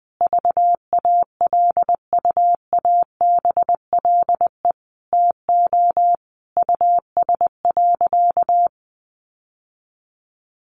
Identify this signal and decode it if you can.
Morse code, heard as a keyed tone, about 20 wpm, 700 Hz. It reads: VALUABLE TO US.
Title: Sentences from Top 1100 Words 20wpm